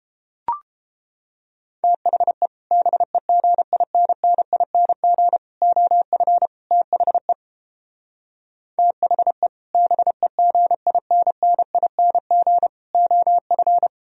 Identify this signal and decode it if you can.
Morse code, heard as a keyed tone, about 29 wpm, 700 Hz. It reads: THE BEGINNING OF THE THE BEGINNING OF